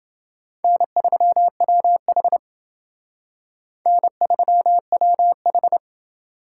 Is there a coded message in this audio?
Yes: Morse code from 0.6 to 5.8 s, heard as a keyed tone, about 29 wpm, 700 Hz. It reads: N3WH N3WH